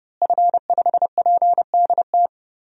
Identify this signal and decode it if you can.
Morse code, heard as a keyed tone, about 30 wpm, 700 Hz. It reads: F5PDT
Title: DXCC-100 Call Signs - 5 Characters - Repeated 2x - 30wpm